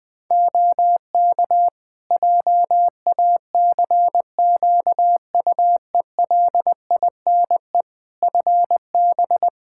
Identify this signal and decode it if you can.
Morse code, heard as a keyed tone, about 20 wpm, 700 Hz. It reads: OK JACQUELINE FB
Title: K1USN SST QSO Practice 20wpm